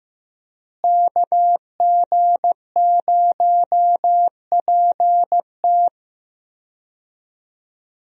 Morse code, heard as a keyed tone, about 16 wpm, 700 Hz. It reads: KG0PT
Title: USA Call Signs - 5 Characters - Repeated 3x - 15wpm